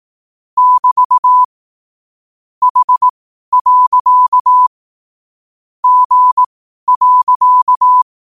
Morse code, heard as a keyed tone, about 18 wpm, spaced as for 12 wpm, 1000 Hz. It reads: <BT> H. G.
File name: qotdfiles / 12wpm